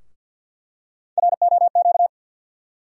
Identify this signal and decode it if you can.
Morse code, heard as a keyed tone, about 50 wpm, 700 Hz. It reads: ROQ
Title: USA Call Sign Postfixes - 3 Letters 50wpm